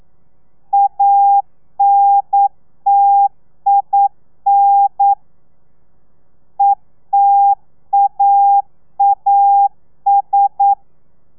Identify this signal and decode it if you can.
Morse code, heard as a keyed tone, about 9 wpm, 800 Hz. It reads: ANTIN ETAAS
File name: telegrafia / i2tfj / Lez2 40